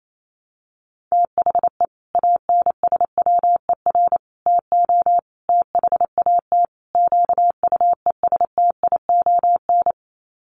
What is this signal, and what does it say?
Morse code, heard as a keyed tone, about 28 wpm, 700 Hz. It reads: THE ANSWER TO THAT QUESTION
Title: Sets of 5 Words Encore2 28wpm